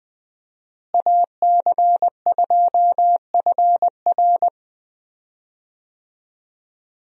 Morse code, heard as a keyed tone, about 20 wpm, 700 Hz. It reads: AC2FR
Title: USA Call Signs - 5 Characters - Repeated 3x - 20wpm